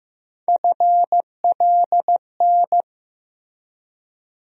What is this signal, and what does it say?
Morse code, heard as a keyed tone, about 15 wpm, 700 Hz. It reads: FLN
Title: USA Call Sign Postfixes - 3 Letters 15wpm